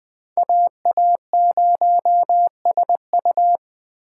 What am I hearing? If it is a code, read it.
Morse code, heard as a keyed tone, about 20 wpm, 700 Hz. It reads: AA0SU
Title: Speed Racing - USA 2x2 Call Signs 17wpm